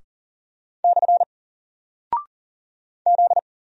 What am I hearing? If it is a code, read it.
Morse code, heard as a keyed tone, about 40 wpm, 700 Hz. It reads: / Z
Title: Single Character (Letter:Number:Punctuation:Prosign) Alt 40wpm